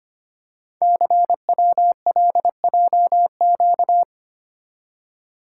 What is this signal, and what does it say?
Morse code, heard as a keyed tone, about 25 wpm, 700 Hz. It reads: CWLJQ
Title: Code Groups - 5 Letters - 25wpm